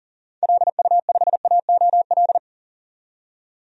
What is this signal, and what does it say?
Morse code, heard as a keyed tone, about 38 wpm, 700 Hz. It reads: LU5AOL
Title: DXCC-100 Call Signs - 6 Characters - 40wpm